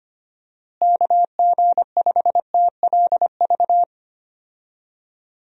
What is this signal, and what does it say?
Morse code, heard as a keyed tone, about 25 wpm, 700 Hz. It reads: KG5TLV
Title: USA Call Signs - 6 Characters - Repeated 3x - 25wpm